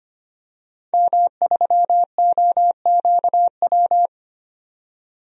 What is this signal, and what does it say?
Morse code, heard as a keyed tone, about 25 wpm, 700 Hz. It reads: M3OQW
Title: DXCC-100 Call Signs - 5 Characters - 25wpm